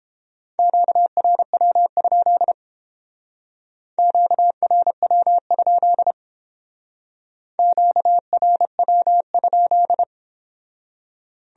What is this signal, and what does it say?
Morse code, heard as a keyed tone, about 30 wpm, 700 Hz. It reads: QRW? QRW? QRW?